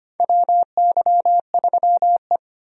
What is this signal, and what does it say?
Morse code, heard as a keyed tone, about 25 wpm, 700 Hz. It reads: WY3E
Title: USA Call Signs - 4 Characters - 25wpm